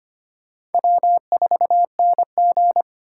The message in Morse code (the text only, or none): W4NG